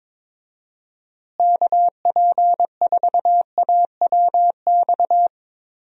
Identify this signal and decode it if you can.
Morse code, heard as a keyed tone, about 22 wpm, 700 Hz. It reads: KP4AWX
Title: DXCC-100 Call Signs - 6 Characters - Repeated 3x - 22wpm